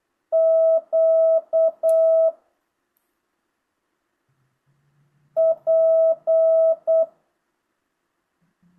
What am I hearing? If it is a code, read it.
Morse code, heard as a keyed tone, about 8 wpm, 660 Hz. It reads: Q P